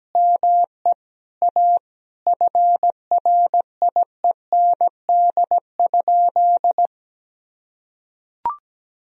Morse code, heard as a keyed tone, about 17 wpm, 700 Hz. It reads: ME A FRIEND?
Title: Sentences from Top 1000 Words - Review Part 4 - 17wpm